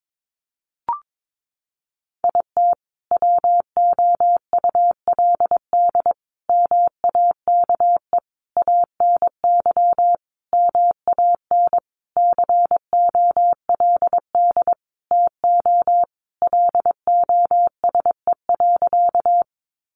Morse code, heard as a keyed tone, about 22 wpm, 700 Hz. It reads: IT WOULD MAKE ANY MAN COLD TO LOSE.
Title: Sentences from Top 1100 Words 22wpm